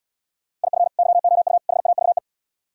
Morse code, heard as a keyed tone, about 76 wpm, 700 Hz. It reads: IF YOU HAVE